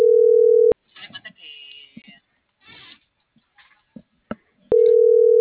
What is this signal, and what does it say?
Ringback tone (North America), 440 + 480 Hz.